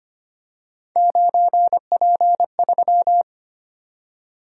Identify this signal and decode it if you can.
Morse code, heard as a keyed tone, about 25 wpm, 700 Hz. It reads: 9P3
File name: Three Letter-Number Combinations - 25wpm